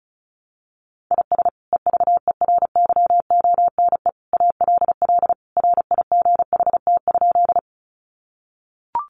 Morse code, heard as a keyed tone, about 34 wpm, 700 Hz. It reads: IS EVERYONE ALL RIGHT?
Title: Sentences from Top 800 Words 35wpm